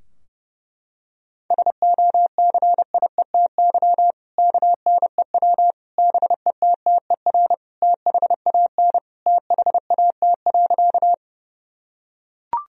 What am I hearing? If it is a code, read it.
Morse code, heard as a keyed tone, about 30 wpm, 700 Hz. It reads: SOCIETY KNEW BETTER THAN THAT.
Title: Sentences from Top 1000 Words - Review Part 4 - 30wpm